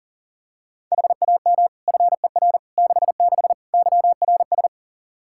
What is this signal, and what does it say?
Morse code, heard as a keyed tone, about 40 wpm, 700 Hz. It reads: HAM FER 66 YRS
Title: Practice QSO - Exchange 3 - More Info - 40wpm